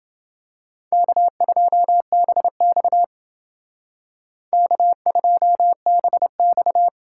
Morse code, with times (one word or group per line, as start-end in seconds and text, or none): 0.9-3.0 K2BX
4.5-6.9 K2BX